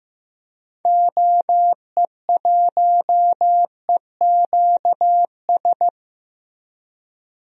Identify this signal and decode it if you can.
Morse code, heard as a keyed tone, about 15 wpm, 700 Hz. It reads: OE1EQS
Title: DXCC-100 Call Signs - 6 Characters - Repeated 3x - 15wpm